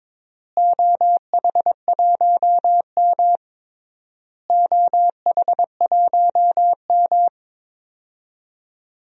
Morse code, heard as a keyed tone, about 22 wpm, 700 Hz. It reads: OH1M OH1M
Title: DXCC-100 Call Signs - 4 Characters - Repeated 3x - 22wpm